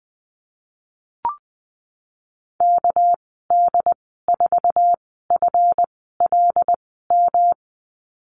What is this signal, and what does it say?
Morse code, heard as a keyed tone, about 20 wpm, spaced as for 15 wpm, 700 Hz. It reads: KD4FLM
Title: 200 Lesson 32g - Letter k - Call Signs 20wpm